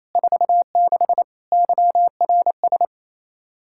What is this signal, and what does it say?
Morse code, heard as a keyed tone, about 28 wpm, 700 Hz. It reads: 46 YRS
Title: Practice QSO - Exchange 3 - More Info - 28wpm